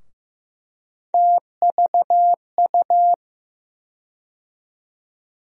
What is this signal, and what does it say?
Morse code, heard as a keyed tone, about 15 wpm, 700 Hz. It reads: TVU
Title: USA Call Sign Postfixes - 3 Letters 15wpm